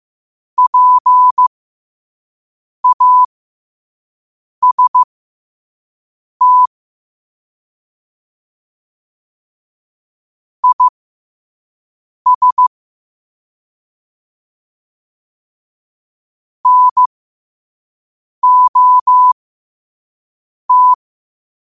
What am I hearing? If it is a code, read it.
Morse code, heard as a keyed tone, about 15 wpm, spaced as for 5 wpm, 1000 Hz. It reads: PAST IS NOT
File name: qotdfiles / 5wpm